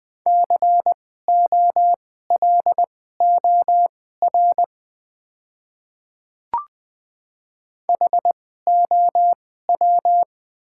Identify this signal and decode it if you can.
Morse code, heard as a keyed tone, about 21 wpm, spaced as for 15 wpm, 700 Hz. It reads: COLOR HOW
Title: 2x Character Spacing - Top 300 Words - 20wpm